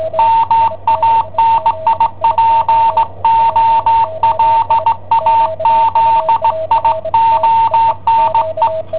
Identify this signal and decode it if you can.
Morse code, heard as a keyed tone, about 16 wpm, 910 Hz, with radio noise. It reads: MANIPOLAZIONE
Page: Cassetta esercizi preparata da I7OHP per IW5ELP